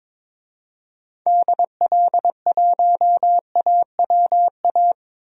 Morse code, heard as a keyed tone, about 22 wpm, 700 Hz. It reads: DL1AWA
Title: DXCC-100 Call Signs - 6 Characters - Repeated 3x - 22wpm